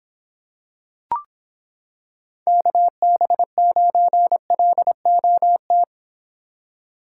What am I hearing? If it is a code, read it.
Morse code, heard as a keyed tone, about 26 wpm, 700 Hz. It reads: KB9LOT